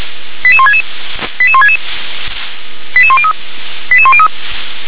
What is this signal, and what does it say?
Five-tone selective call (ZVEI), heard as quick runs of five tones, tones of about 70 ms. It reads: 88166 88166 88103 88103